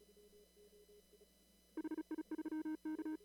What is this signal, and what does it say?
Morse code, heard as a keyed tone, about 35 wpm, 310 Hz. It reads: HI3K
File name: CW / kt5j wpx cw-op-k5pi